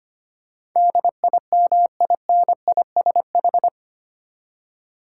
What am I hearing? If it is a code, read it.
Morse code, heard as a keyed tone, about 25 wpm, 700 Hz. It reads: DIMINISH
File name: Eight Letter Words - 25wpm